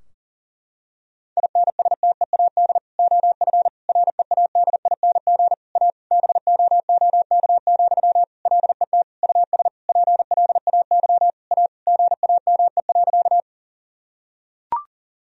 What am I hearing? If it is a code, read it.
Morse code, heard as a keyed tone, about 40 wpm, 700 Hz. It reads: INSTEAD OF READING A BOOK, LET US PLAY A GAME.
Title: Sentences from Top 1000 Words - Review Part 4 - 40wpm